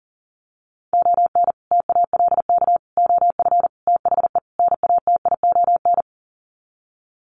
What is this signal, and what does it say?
Morse code, heard as a keyed tone, about 37 wpm, 700 Hz. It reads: ON TALK OF THE NATION
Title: Sets of 5 Words 40wpm